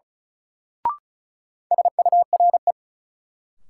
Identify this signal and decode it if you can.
Morse code, heard as a keyed tone, about 35 wpm, 700 Hz. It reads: SURE
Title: Sentences from Top 300 Words - New Word Drill - 35wpm